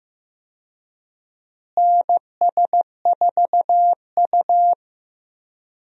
Morse code, heard as a keyed tone, about 15 wpm, 700 Hz. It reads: NS4U
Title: USA Call Signs - 4 Characters - Repeated 3x - 15wpm